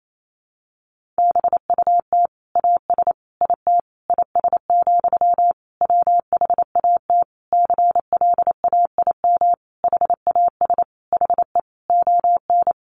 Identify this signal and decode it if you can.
Morse code, heard as a keyed tone, about 28 wpm, 700 Hz. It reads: BUT AS IT IS, WHAT CLAIM HAS HE ON
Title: Sentences from Top 1500 Words 28wpm